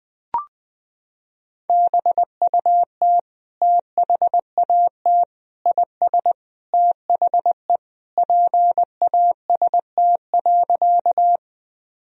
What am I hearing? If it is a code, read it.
Morse code, heard as a keyed tone, about 20 wpm, 700 Hz. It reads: BUT THAT IS THE PAST.
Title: Sentences from Top 1000 Words - Review Part 3 - 20wpm